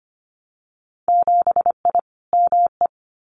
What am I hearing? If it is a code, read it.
Morse code, heard as a keyed tone, about 25 wpm, 700 Hz. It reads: 7I ME
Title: Winter Field Day Exchanges Encore 25wpm